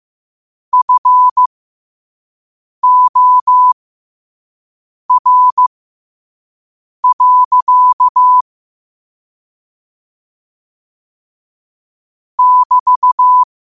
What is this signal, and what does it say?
Morse code, heard as a keyed tone, about 15 wpm, spaced as for 5 wpm, 1000 Hz. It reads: FOR. <BT>
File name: qotdfiles / 5wpm